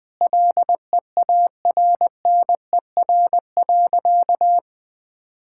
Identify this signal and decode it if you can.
Morse code, heard as a keyed tone, about 20 wpm, 700 Hz. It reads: LEARNER.
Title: Sentences with 8-Word Phrases - Spoken Beginning - 20wpm